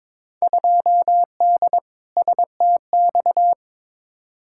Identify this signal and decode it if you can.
Morse code, heard as a keyed tone, about 22 wpm, 700 Hz. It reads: 2D STX
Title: Field Day Exchanges - 22wpm